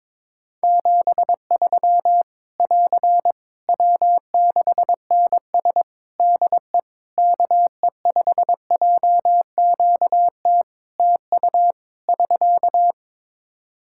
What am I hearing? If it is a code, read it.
Morse code, heard as a keyed tone, about 22 wpm, 700 Hz. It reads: 73 <AR> W6NS DE KE5JQT TU <SK>
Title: Practice QSO - Exchange 4 - Ending - 22wpm